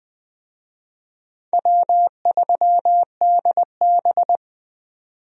Morse code, heard as a keyed tone, about 20 wpm, 700 Hz. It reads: W3DB
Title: USA Call Signs - 4 Characters - 20wpm